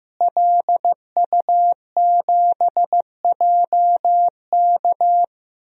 Morse code, heard as a keyed tone, about 15 wpm, 700 Hz. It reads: LU7JK
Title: DXCC-100 Call Signs - 5 Characters - Repeated 2x - 15wpm